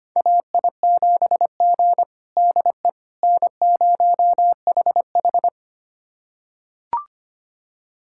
Morse code, heard as a keyed tone, about 25 wpm, 700 Hz. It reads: AI7G DE N0HH
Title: USA Callsign DE Callsign - 4 Characters - 25wpm